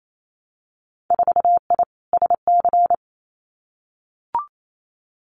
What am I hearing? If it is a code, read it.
Morse code, heard as a keyed tone, about 28 wpm, 700 Hz. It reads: 4I SC
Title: Winter Field Day Exchanges 28wpm